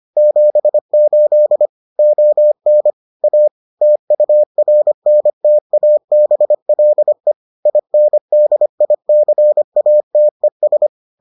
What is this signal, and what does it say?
Morse code, heard as a keyed tone, about 25 wpm, 600 Hz. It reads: 78 ON A TURNTABLE INDICATES